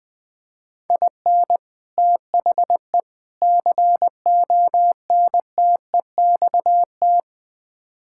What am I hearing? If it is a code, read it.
Morse code, heard as a keyed tone, about 21 wpm, 700 Hz. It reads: IN THE CONTEXT
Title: Build A Phrase - 4-Word Phrases - 20wpm